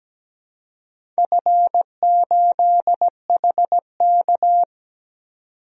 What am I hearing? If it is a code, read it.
Morse code, heard as a keyed tone, about 17 wpm, 700 Hz. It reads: F8HK
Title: DXCC-100 Call Signs - 4 Characters - 17wpm